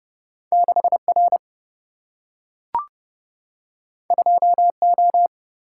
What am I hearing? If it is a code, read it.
Morse code, heard as a keyed tone, about 31 wpm, 700 Hz. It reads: 6R 2O
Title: Single-Letter Single-Number 30wpm